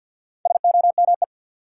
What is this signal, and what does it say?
Morse code, heard as a keyed tone, about 54 wpm, 700 Hz. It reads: SOME
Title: Sentences from Top 100 Words - New Word Drill - 50wpm